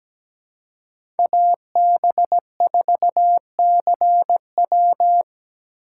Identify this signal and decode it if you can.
Morse code, heard as a keyed tone, about 17 wpm, 700 Hz. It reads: AB4CW
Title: USA Call Signs - 5 Characters - 8x Character Spacing - 17wpm